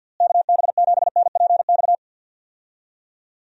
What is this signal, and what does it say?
Morse code, heard as a keyed tone, about 46 wpm, 700 Hz. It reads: KB7NPX
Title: USA Call Signs - 6 Characters - Repeated 3x - 50wpm